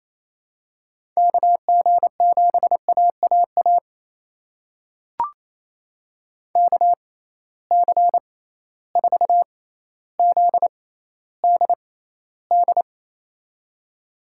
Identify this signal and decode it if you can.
Morse code, heard as a keyed tone, about 28 wpm, spaced as for 14 wpm, 700 Hz. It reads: KG7AAA KC4ZDD
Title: USA Call Signs - 6 Characters - 6x Character Spacing - 28wpm